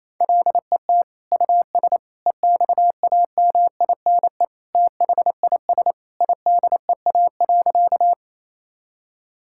Morse code, heard as a keyed tone, about 28 wpm, 700 Hz. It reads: LET US EXAMINE THIS IDEA.
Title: Sentences from Top 1600 Words 28wpm